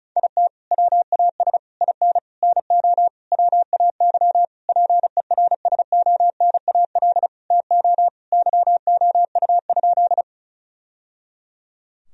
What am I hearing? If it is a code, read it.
Morse code, heard as a keyed tone, about 34 wpm, 700 Hz. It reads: IT WAS IN NO WAY PERSONAL TO YOU?
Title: Sentences from Top 1000 Words - Review Part 4 - 35wpm